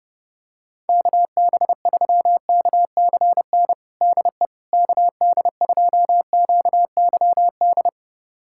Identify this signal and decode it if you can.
Morse code, heard as a keyed tone, about 30 wpm, 700 Hz. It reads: KB3KCN DE KD2QYD
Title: USA Callsign DE Callsign - 6 Characters - 30wpm